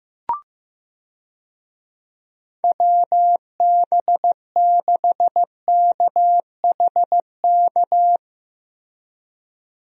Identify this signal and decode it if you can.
Morse code, heard as a keyed tone, about 15 wpm, 700 Hz. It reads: WB6KHK
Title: USA Call Signs - 6 Characters - Repeated 3x - 15wpm